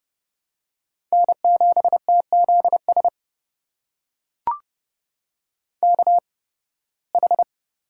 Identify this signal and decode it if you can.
Morse code, heard as a keyed tone, about 32 wpm, spaced as for 20 wpm, 700 Hz. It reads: N7TZS KH
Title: USA Call Signs - 5 Characters - 8x Character Spacing - 30wpm